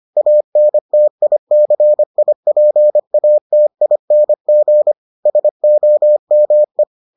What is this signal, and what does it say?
Morse code, heard as a keyed tone, about 25 wpm, 600 Hz. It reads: ANTICIPATING SOME